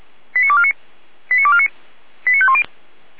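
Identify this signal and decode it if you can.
Five-tone selective call (ZVEI), heard as quick runs of five tones, tones of about 70 ms. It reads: 89249 89249 89510
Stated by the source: ZVEI five-tone call received by SDR, served by feuerwehr-orlamuende.org fire brigade.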